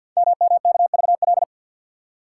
Morse code, heard as a keyed tone, about 52 wpm, 700 Hz. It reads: MMKVL
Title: Code Groups - 5 Letters - 50wpm